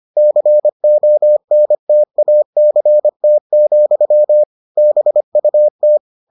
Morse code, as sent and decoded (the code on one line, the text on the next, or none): -.-. --- -. - .- -.-. - --..-- / -... ..- -
CONTACT, BUT